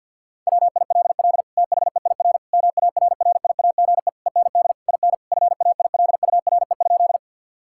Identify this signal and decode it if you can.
Morse code, heard as a keyed tone, about 50 wpm, 700 Hz. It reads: WILL THEIR MARRIAGE END IN FAILURE?
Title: Sentences from Top 1000 Words - Review Part 6 - 50wpm